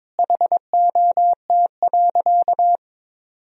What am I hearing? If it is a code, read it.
Morse code, heard as a keyed tone, about 22 wpm, 700 Hz. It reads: HOT.
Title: Sentences from Top 1000 Words - Review Part 3 - 22wpm